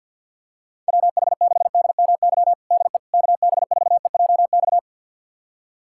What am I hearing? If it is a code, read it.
Morse code, heard as a keyed tone, about 51 wpm, 700 Hz. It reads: WH6DMY DE KB4EJX
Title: USA Callsign DE Callsign - 6 Characters - 50wpm